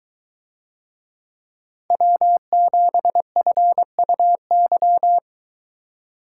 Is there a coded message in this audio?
Yes: Morse code from 1.9 to 5.2 s, heard as a keyed tone, about 23 wpm, 700 Hz. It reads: W7FUY